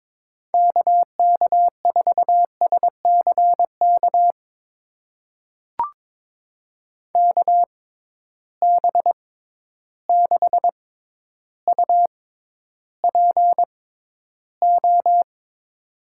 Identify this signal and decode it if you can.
Morse code, heard as a keyed tone, about 22 wpm, spaced as for 11 wpm, 700 Hz. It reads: KK4SCK KB6UPO
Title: USA Call Signs - 6 Characters - 6x Character Spacing - 22wpm